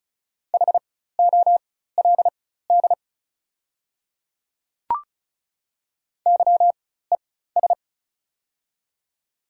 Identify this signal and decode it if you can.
Morse code, heard as a keyed tone, about 36 wpm, spaced as for 16 wpm, 690 Hz. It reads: HOLD YES
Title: 4x Character Spacing - Top 500 Words - 35wpm